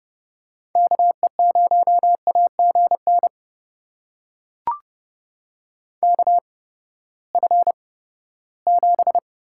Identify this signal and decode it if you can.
Morse code, heard as a keyed tone, about 32 wpm, spaced as for 17 wpm, 700 Hz. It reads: KE0AGN KF7
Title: USA Call Signs - 8x Character Spacing - 30wpm